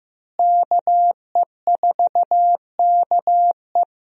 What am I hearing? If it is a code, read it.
Morse code, heard as a keyed tone, about 15 wpm, 700 Hz. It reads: KE4KE
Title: USA Call Signs - 5 Characters - Repeated 2x - 15wpm